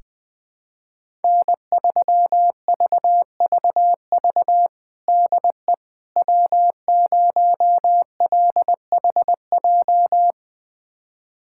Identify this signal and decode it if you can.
Morse code, heard as a keyed tone, about 21 wpm, 700 Hz. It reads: N3VVV DE W0LHJ
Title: USA Callsign DE Callsign - 5 Characters - 20wpm